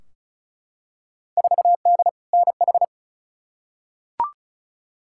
Morse code, heard as a keyed tone, about 36 wpm, 700 Hz. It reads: 4D NH
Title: Field Day Exchanges Encore - 35wpm